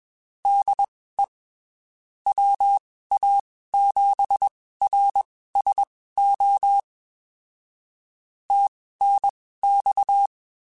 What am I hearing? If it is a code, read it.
Morse code, heard as a keyed tone, about 21 wpm, spaced as for 15 wpm, 800 Hz. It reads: DE WA7RSO TNX
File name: (spaced) / QSOs Spaced-mp3s / kc7haa @20 sp